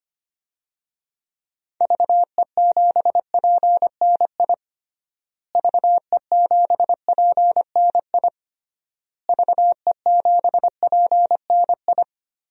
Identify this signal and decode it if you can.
Morse code, heard as a keyed tone, about 25 wpm, 700 Hz. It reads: VE7PNI VE7PNI VE7PNI
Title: DXCC-100 Call Signs - 6 Characters - Repeated 3x - 25wpm